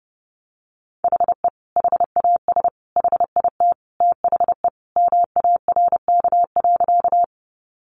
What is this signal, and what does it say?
Morse code, heard as a keyed tone, about 30 wpm, 700 Hz. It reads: HE HAS HIT THE MARK.
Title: Sentences from Top 1200 Words 30wpm